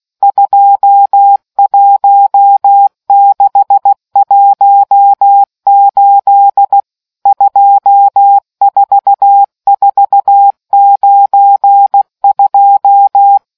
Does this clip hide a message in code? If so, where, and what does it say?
Yes: Morse code from 0.2 to 13.4 s, heard as a keyed tone, about 16 wpm, 800 Hz. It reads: 21618 24492